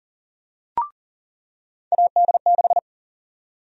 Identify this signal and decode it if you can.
Morse code, heard as a keyed tone, about 40 wpm, 700 Hz. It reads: AD6
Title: USA Call Sign Prefixes - 40wpm